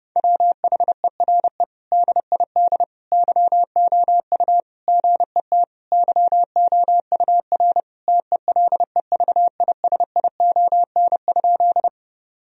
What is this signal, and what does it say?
Morse code, heard as a keyed tone, about 32 wpm, 700 Hz. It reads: WHERE DID YOU GET YOUR TELEVISION?
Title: Sentences from Top 1000 Words - Review Part 4 - 30wpm